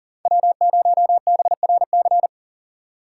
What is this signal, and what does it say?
Morse code, heard as a keyed tone, about 51 wpm, 700 Hz. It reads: W0BRC